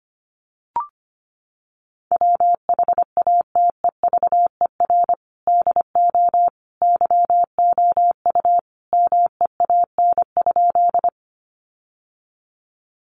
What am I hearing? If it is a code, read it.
Morse code, heard as a keyed tone, about 25 wpm, 700 Hz. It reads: WHATEVER DO YOU MEAN?
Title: Sentences from Top 1100 Words 25wpm